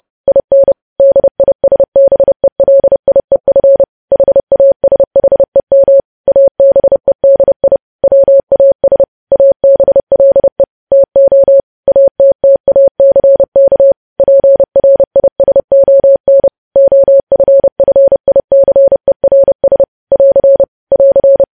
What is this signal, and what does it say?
Morse code, heard as a keyed tone, about 30 wpm, 550 Hz. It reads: IN DISBELIEF HASHEM ABEDI WAS ABLE TO ATTACK PRISON OFFICERS <AR> <AR>
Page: news headlines in morse code at 30 wpm